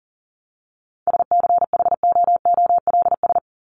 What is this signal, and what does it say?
Morse code, heard as a keyed tone, about 40 wpm, 700 Hz. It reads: SCHOOLS
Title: ES:S Suffix - Top 100 Words with Base Words 40wpm